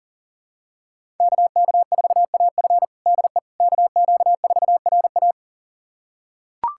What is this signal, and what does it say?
Morse code, heard as a keyed tone, about 37 wpm, 700 Hz. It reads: KK4AF DE KQ4RA
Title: USA Callsign DE Callsign - 5 Characters - 40wpm